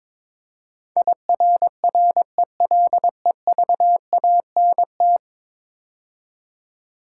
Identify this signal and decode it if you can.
Morse code, heard as a keyed tone, about 21 wpm, 700 Hz. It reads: IRRELEVANT
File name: practice / 22wpm / 10 Letter Words - First 3 Letters Given - 22wpm